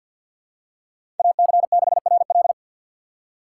Morse code, heard as a keyed tone, about 47 wpm, 700 Hz. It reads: AC6RL
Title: USA Call Signs - 5 Characters - 6x Character Spacing - 50wpm